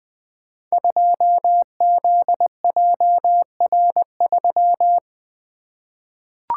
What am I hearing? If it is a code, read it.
Morse code, heard as a keyed tone, about 20 wpm, 700 Hz. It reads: 2ZJR3
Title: Code Groups - 5 Letter:Numbers - 20wpm